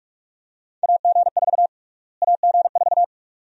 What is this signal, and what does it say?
Morse code, heard as a keyed tone, about 46 wpm, 690 Hz. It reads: AG4 AG4
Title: USA Call Sign Prefixes - Repeated 2x - 45wpm